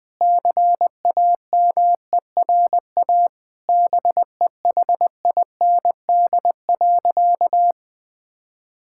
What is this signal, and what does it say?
Morse code, heard as a keyed tone, about 20 wpm, 700 Hz. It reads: CAMERA BEHIND.
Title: Sentences from Top 1000 Words - Review Part 5 - 20wpm